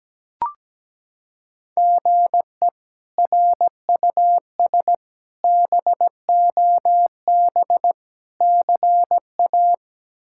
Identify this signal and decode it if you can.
Morse code, heard as a keyed tone, about 17 wpm, 700 Hz. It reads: GE RUS BOB CA
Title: K1USN SST QSO Practice - 17wpm